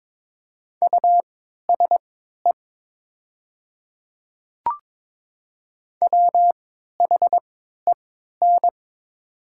Morse code, heard as a keyed tone, about 22 wpm, spaced as for 13 wpm, 700 Hz. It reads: USE WHEN
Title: 3x Character Spacing - Top 200 Words - 22wpm